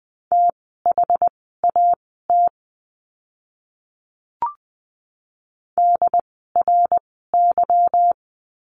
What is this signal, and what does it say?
Morse code, heard as a keyed tone, about 20 wpm, spaced as for 15 wpm, 700 Hz. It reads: THAT DRY
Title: 146 Lesson 24f - Letter y - Words 20wpm